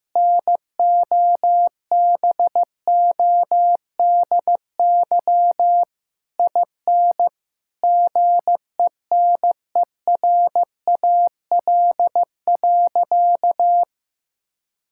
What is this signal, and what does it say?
Morse code, heard as a keyed tone, about 15 wpm, 700 Hz. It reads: NOBODY IN GENERAL.
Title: Sentences from Top 1600 Words 15wpm